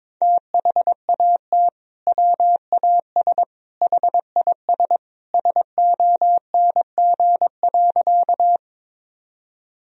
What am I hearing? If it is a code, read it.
Morse code, heard as a keyed tone, about 21 wpm, 700 Hz. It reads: THAT WAS HIS SONG.
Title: Sentences from Top 1000 Words - Review Part 2 - 22wpm